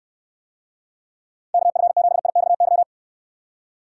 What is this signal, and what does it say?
Morse code, heard as a keyed tone, about 68 wpm, 700 Hz. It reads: KF7IXY